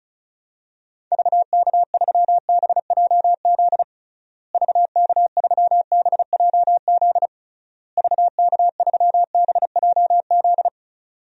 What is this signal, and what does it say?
Morse code, heard as a keyed tone, about 36 wpm, 700 Hz. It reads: VK3BJZ VK3BJZ VK3BJZ
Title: DXCC-100 Call Signs - 6 Characters - Repeated 3x - 35wpm